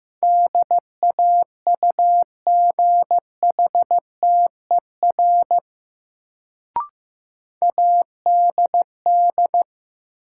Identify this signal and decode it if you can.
Morse code, heard as a keyed tone, about 15 wpm, 700 Hz. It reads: DAUGHTER ADD
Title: Sentences from Top 1500 Words - New Word Drill Alt - 15wpm